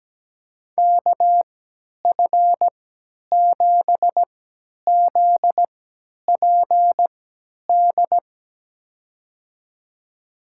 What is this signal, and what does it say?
Morse code, heard as a keyed tone, about 17 wpm, spaced as for 10 wpm, 700 Hz. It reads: KF7ZPD
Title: USA Call Signs - 6 Characters - 3x Character Spacing - 17wpm